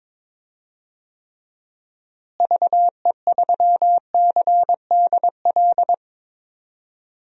Morse code, heard as a keyed tone, about 22 wpm, 700 Hz. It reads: VE3CDL
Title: DXCC-100 Call Signs - 6 Characters - Repeated 2x - 22wpm